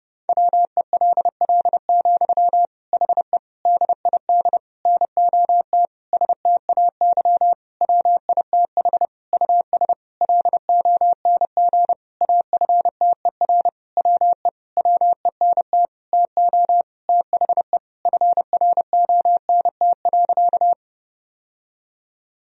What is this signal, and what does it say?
Morse code, heard as a keyed tone, about 30 wpm, 700 Hz. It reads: WELL, HE DID NOT STAY WITH US LONG AFTER WE WENT TO THE FRONT.
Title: Sentences from Top 1000 Words - Review Part 2 - 30wpm